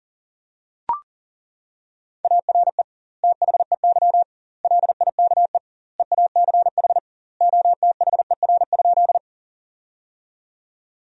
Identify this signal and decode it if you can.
Morse code, heard as a keyed tone, about 37 wpm, 700 Hz. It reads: ARE THEY LIKE EACH OTHER?
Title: 2x Word-Spacing - Sentences from Top 200 Words - 40wpm